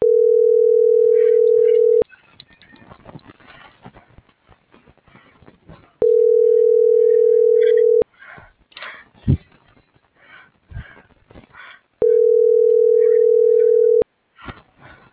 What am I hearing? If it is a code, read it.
Ringback tone (North America), 440 + 480 Hz.